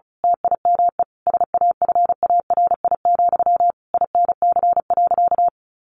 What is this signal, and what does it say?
Morse code, heard as a keyed tone, about 34 wpm, 700 Hz. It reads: TIME SAFARI, INC.
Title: A Sound Of Thunder 35wpm